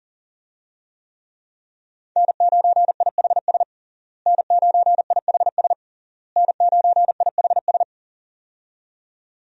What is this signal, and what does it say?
Morse code, heard as a keyed tone, about 40 wpm, 700 Hz. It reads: N9IHS N9IHS N9IHS
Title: USA Call Signs - 5 Characters - Repeated 3x - 40wpm